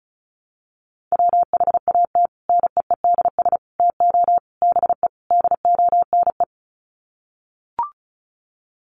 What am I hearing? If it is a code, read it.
Morse code, heard as a keyed tone, about 34 wpm, 700 Hz. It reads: WHAT NEEDS TO BE DONE
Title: Sets of 5 Words Encore 35wpm